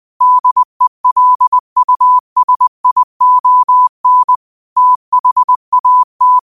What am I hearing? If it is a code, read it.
Morse code, heard as a keyed tone, about 20 wpm, 1000 Hz. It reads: DELUSION THAT
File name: qotdfiles / 20wpm